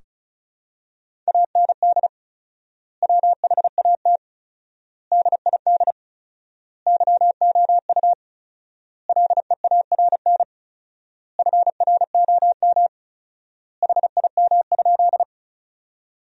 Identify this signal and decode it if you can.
Morse code, heard as a keyed tone, about 34 wpm, 700 Hz. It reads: AND WHAT DID YOU LEARN FROM HIM?
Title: 4x Word-Spacing - Sentences from Top 300 Words - 35wpm